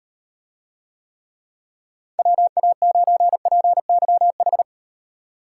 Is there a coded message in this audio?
Yes: Morse code from 2.2 to 4.6 s, heard as a keyed tone, about 38 wpm, 700 Hz. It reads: WA9PYH